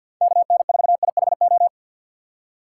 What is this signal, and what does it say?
Morse code, heard as a keyed tone, about 49 wpm, 700 Hz. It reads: KN4IHO
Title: USA Call Signs - 6 Characters - 3x Character Spacing - 50wpm